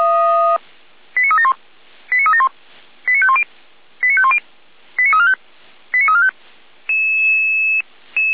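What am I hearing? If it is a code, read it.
Five-tone selective call (ZVEI), heard as quick runs of five tones, tones of about 70 ms. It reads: 89371 89371 89510 89510 89345 89345
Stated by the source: ZVEI five-tone selective call received via SDR.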